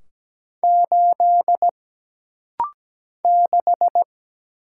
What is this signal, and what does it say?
Morse code, heard as a keyed tone, about 17 wpm, 700 Hz. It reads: 8 6
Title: Single Letter-Number Alt - Rapid-Fire 17wpm